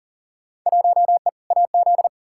Morse code, heard as a keyed tone, about 38 wpm, 700 Hz. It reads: 1E AZ
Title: Field Day Exchanges - 40wpm